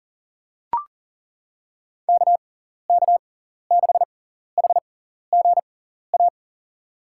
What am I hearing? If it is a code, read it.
Morse code, heard as a keyed tone, about 40 wpm, spaced as for 14 wpm, 700 Hz. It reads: KK6HGA
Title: USA Call Signs - 6 Characters - 6x Character Spacing - 40wpm